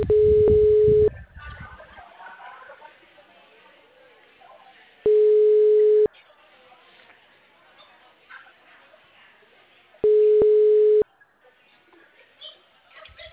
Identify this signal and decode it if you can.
Ringback tone (Europe, 425 Hz), with speech.